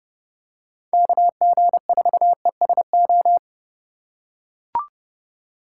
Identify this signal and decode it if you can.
Morse code, heard as a keyed tone, about 30 wpm, 700 Hz. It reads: KG4ESO